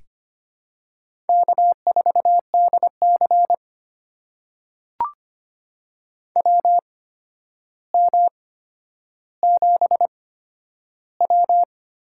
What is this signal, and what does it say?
Morse code, heard as a keyed tone, about 25 wpm, spaced as for 11 wpm, 700 Hz. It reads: K4DC WM7W
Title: USA Call Signs - 4 Characters - 8x Character Spacing - 25wpm